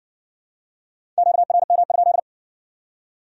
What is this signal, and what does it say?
Morse code, heard as a keyed tone, about 56 wpm, 700 Hz. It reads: QRM?